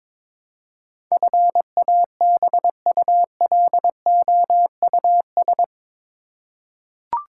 Morse code, heard as a keyed tone, about 21 wpm, 700 Hz. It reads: FABULOUS
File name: practice / 22wpm / Eight Letter Words - 22wpm